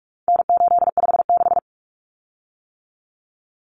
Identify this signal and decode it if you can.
Morse code, heard as a keyed tone, about 44 wpm, 700 Hz. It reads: N856
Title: Common Cut Numbers - 4-Digit 45wpm